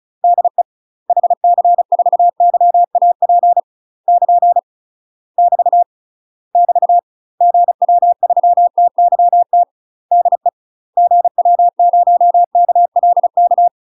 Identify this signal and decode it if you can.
Morse code, heard as a keyed tone, about 36 wpm, 700 Hz. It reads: DE HC4YAP <KN> <BT> <BT> GW3TYT DE GW0KLK
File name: UK MASTER QSOs TRAINING / 35WPM / UK MASTER QSOs 0013